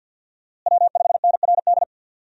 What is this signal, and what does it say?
Morse code, heard as a keyed tone, about 50 wpm, 700 Hz. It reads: W5NRD
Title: USA Call Signs - 5 Characters - Repeated 2x - 50wpm